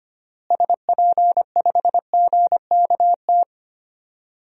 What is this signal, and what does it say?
Morse code, heard as a keyed tone, about 25 wpm, 700 Hz. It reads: SP5GKT
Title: DXCC-100 Call Signs - 6 Characters - Repeated 3x - 25wpm